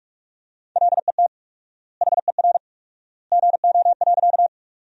Morse code, heard as a keyed tone, about 46 wpm, 700 Hz. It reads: LET HER GO.
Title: 4x Word-Spacing - Sentences from Top 300 Words - 45wpm